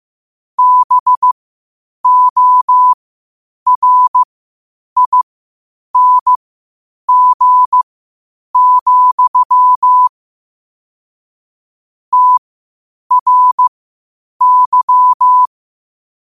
Morse code, heard as a keyed tone, about 15 wpm, spaced as for 8 wpm, 1000 Hz. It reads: BORING, TRY
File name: qotdfiles / 8wpm